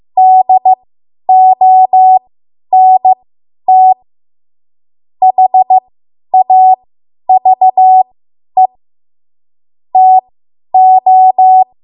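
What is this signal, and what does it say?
Morse code, heard as a keyed tone, about 15 wpm, spaced as for 10 wpm, 750 Hz. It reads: DONT HAVE TO
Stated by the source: Morse code practice run at 10 WPM from a code practice archive.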